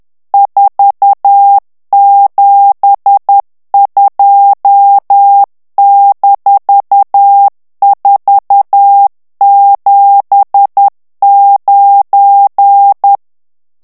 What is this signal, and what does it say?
Morse code, heard as a keyed tone, about 10 wpm, 800 Hz. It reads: 472-479
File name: ANGLAIS texte clair 3 vitesse 10 mots minute